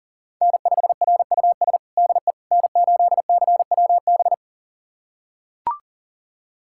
Morse code, heard as a keyed tone, about 40 wpm, 700 Hz. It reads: N5RUS DE N8CWB
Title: USA Callsign DE Callsign - 5 Characters - 40wpm